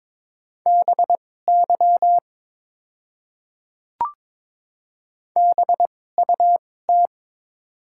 Morse code, heard as a keyed tone, about 22 wpm, spaced as for 16 wpm, 700 Hz. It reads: BY BUT
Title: 2x Character Spacing - Top 100 Words - 22wpm